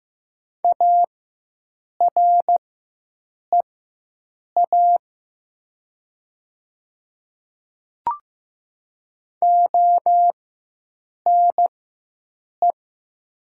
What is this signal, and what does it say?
Morse code, heard as a keyed tone, about 15 wpm, spaced as for 7 wpm, 700 Hz. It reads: AREA ONE
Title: 4x Character Spacing - Top 500 Words - 15wpm